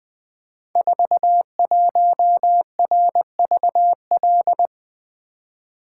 Morse code, heard as a keyed tone, about 20 wpm, 700 Hz. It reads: 41RVL
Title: Code Groups - 5 Letter:Numbers - 20wpm